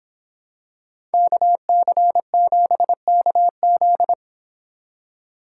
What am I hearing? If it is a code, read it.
Morse code, heard as a keyed tone, about 26 wpm, 700 Hz. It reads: KC7KZ